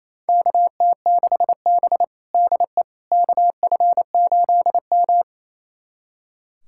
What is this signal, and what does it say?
Morse code, heard as a keyed tone, about 28 wpm, 700 Hz. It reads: KT6B DE KF8M
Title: USA Callsign DE Callsign - 4 Characters - 28wpm